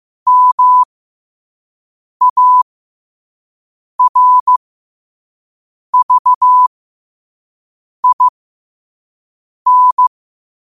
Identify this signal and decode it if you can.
Morse code, heard as a keyed tone, about 15 wpm, spaced as for 5 wpm, 1000 Hz. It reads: MARVIN
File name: qotdfiles / 5wpm